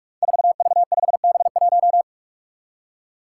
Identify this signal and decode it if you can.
Morse code, heard as a keyed tone, about 45 wpm, 700 Hz. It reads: 4V5B1
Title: Five Challenging Letter-Number Combinations 45wpm